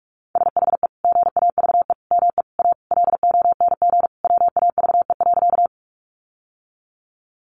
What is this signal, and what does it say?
Morse code, heard as a keyed tone, about 44 wpm, 700 Hz. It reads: SHE GAVE ME A LONG WAVE.
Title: Sentences from Top 1400 Words 45wpm